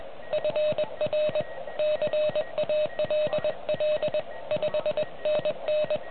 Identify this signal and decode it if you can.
Morse code, heard as a keyed tone, about 20 wpm, 610 Hz, with noise. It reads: FR CALL 5NN